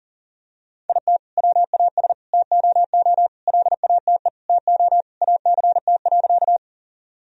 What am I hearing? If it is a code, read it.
Morse code, heard as a keyed tone, about 40 wpm, 700 Hz. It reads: IT WAS TOO LATE TO ACT.
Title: Sentences from Top 1000 Words - Review Part 1 - 40wpm